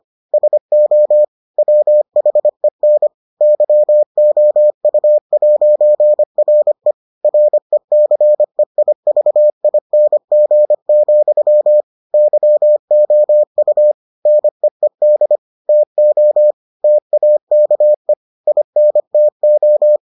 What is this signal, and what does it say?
Morse code, heard as a keyed tone, about 25 wpm, 600 Hz. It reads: SO WHEN YOU'RE RECEIVING, YOU NEED TO TAKE INTO